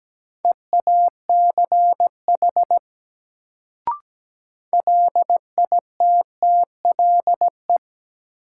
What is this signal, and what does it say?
Morse code, heard as a keyed tone, about 17 wpm, 700 Hz. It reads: EACH LITTLE
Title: Sentences from Top 200 Words - New Word Drill - 17wpm